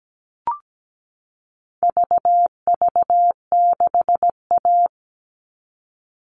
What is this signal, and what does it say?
Morse code, heard as a keyed tone, about 17 wpm, 700 Hz. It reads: VV6A
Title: All Cut Numbers - 4-Digit 17wpm